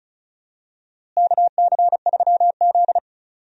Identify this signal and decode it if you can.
Morse code, heard as a keyed tone, about 34 wpm, 700 Hz. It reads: KC3Z
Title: USA Call Signs - 4 Characters - Repeated 3x - 35wpm